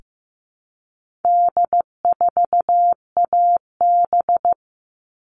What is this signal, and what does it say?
Morse code, heard as a keyed tone, about 15 wpm, 700 Hz. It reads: D4AB
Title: All Cut Numbers - 4-Digit 15wpm